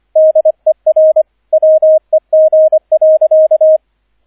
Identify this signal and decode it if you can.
Morse code, heard as a keyed tone, about 25 wpm, 630 Hz, with radio noise. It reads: DER WEG.
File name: 003 Dritter Streich 25wpm